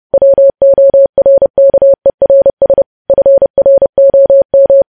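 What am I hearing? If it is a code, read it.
Morse code, heard as a keyed tone, about 30 wpm, 550 Hz. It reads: WORKERS FROM
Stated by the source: Morse code news headlines sent at 30 words per minute.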